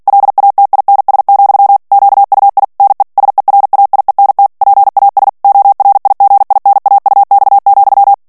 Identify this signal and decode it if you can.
Morse code, heard as a keyed tone, about 48 wpm, 800 Hz. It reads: LATINS, QUI NE SERAIENT PAS ORIGINAUX,
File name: CW / FICHIERS-MP3 / FRANCAIS TEXTE CLAIR 2 VITESSE 45 MOTS MINUTES